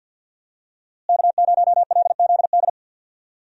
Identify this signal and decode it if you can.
Morse code, heard as a keyed tone, about 52 wpm, 700 Hz. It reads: K0LZD
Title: USA Call Signs - 5 Characters - Repeated 3x - 50wpm